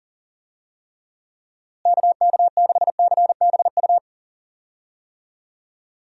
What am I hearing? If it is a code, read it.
Morse code, heard as a keyed tone, about 40 wpm, 700 Hz. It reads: KK6CBU
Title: USA Call Signs - 6 Characters - Repeated 3x - 40wpm